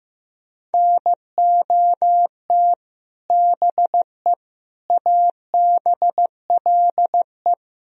Morse code, heard as a keyed tone, about 16 wpm, 700 Hz. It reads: NOT BE ABLE
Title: Speed-Racing - Sets of 4 Words 15wpm